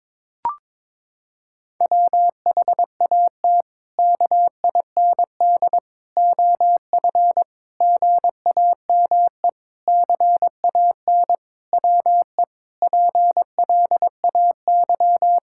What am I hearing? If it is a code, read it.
Morse code, heard as a keyed tone, about 22 wpm, 700 Hz. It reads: WHAT KIND OF GAME CAN WE PLAY
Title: Sentences from Top 1000 Words - Review Part 6 - 22wpm